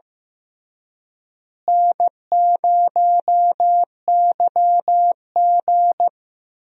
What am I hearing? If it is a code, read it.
Morse code, heard as a keyed tone, about 15 wpm, 700 Hz. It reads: N0YG
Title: USA Call Signs - 4 Characters - Repeated 2x - 15wpm